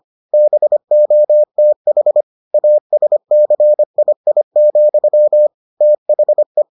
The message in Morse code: BOTH ASCII, THE

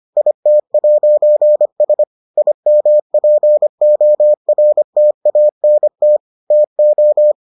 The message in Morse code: IT'S IMPORTANT TO

